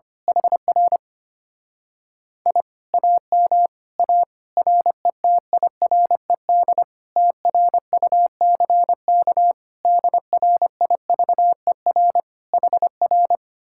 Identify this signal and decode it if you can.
Morse code, heard as a keyed tone, about 24 wpm, 700 Hz. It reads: HR I AM A RETIRED TRUCK DRIVER HR
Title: Speed-Racing Common QSO Phrases - 25wpm